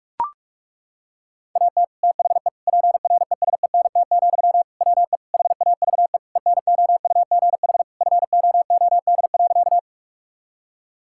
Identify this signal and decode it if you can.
Morse code, heard as a keyed tone, about 41 wpm, 700 Hz. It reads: AT THE PRESENT, WE HAVE ENOUGH FOOD.
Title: Sentences from Top 1000 Words - Review Part 5 - 45wpm